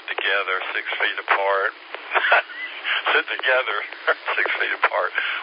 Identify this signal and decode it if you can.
Emergency Alert System (SAME) tones, heard as harsh buzzing bursts, with speech. It is heard from 0.7 to 4.7 s.